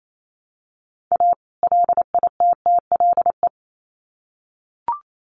Morse code, heard as a keyed tone, about 28 wpm, 700 Hz. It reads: A LITTLE
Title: Sets of 2 Words 28wpm